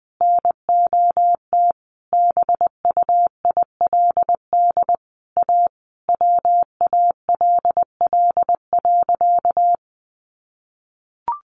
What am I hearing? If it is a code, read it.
Morse code, heard as a keyed tone, about 19 wpm, 700 Hz. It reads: NOT BUILD A WALL.
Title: Sentences from Top 1100 Words 20wpm